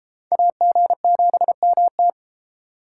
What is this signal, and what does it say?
Morse code, heard as a keyed tone, about 33 wpm, 700 Hz. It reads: AG7MT